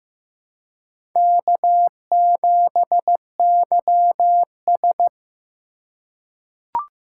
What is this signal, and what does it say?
Morse code, heard as a keyed tone, about 15 wpm, 700 Hz. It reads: K7YS
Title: USA Call Signs - 4 Characters - 4x Character Spacing - 15wpm